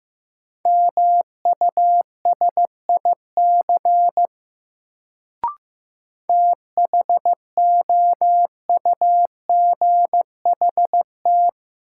Morse code, heard as a keyed tone, about 15 wpm, 700 Hz. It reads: MUSIC THOUGHT
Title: Sentences from Top 300 Words - New Word Drill - 15wpm